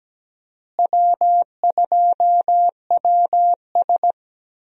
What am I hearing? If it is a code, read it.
Morse code, heard as a keyed tone, about 17 wpm, 700 Hz. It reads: W2WS
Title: USA Call Signs - 4 Characters - Repeated 3x - 17wpm